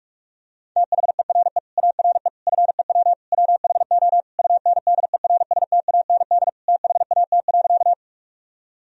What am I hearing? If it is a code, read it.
Morse code, heard as a keyed tone, about 46 wpm, 700 Hz. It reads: THERE ARE FEW WHO UNDERSTAND THAT.
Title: Sentences from Top 1000 Words - Review Part 1 - 45wpm